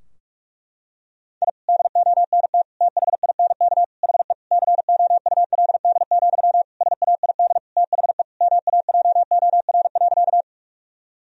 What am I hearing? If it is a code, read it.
Morse code, heard as a keyed tone, about 46 wpm, 700 Hz. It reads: I DONT THINK HE COULD, SAID THE MAJOR.
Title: Sentences from Top 1000 Words - Review Part 4 - 45wpm